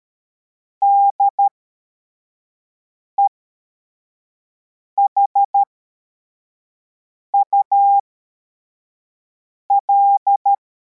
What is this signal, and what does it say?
Morse code, heard as a keyed tone, about 13 wpm, spaced as for 4 wpm, 800 Hz. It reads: DEHUL
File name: formation / cw / Text81